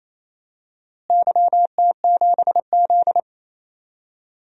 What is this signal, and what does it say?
Morse code, heard as a keyed tone, about 27 wpm, 700 Hz. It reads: YT7Z
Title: DXCC-100 Call Signs - 4 Characters - Repeated 2x - 28wpm